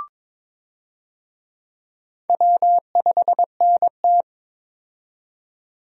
Morse code, heard as a keyed tone, about 22 wpm, 700 Hz. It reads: W5NT
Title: USA Call Signs - 4 Characters - 22wpm